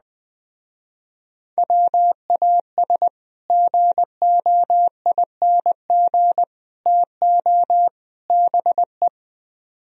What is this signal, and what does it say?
Morse code, heard as a keyed tone, about 20 wpm, 700 Hz. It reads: WAS GOING TO BE
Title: Speed-Racing - Sets of 4 Words 17wpm